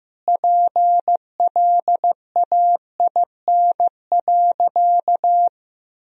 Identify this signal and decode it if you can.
Morse code, heard as a keyed tone, about 15 wpm, 700 Hz. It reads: PLAIN.
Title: Sentences from Top 400 Words and Abbreviations 15wpm